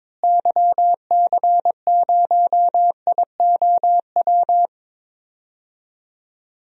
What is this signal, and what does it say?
Morse code, heard as a keyed tone, about 21 wpm, 700 Hz. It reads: YC0IOW
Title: DXCC-100 Call Signs - 6 Characters - Repeated 3x - 22wpm